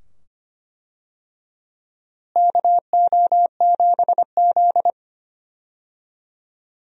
Morse code, heard as a keyed tone, about 25 wpm, 700 Hz. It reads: KO7Z
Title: USA Call Signs - 4 Characters - Repeated 2x - 25wpm